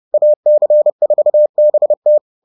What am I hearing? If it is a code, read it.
Morse code, heard as a keyed tone, about 30 wpm, 600 Hz. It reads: AC4BT
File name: 30wpm-7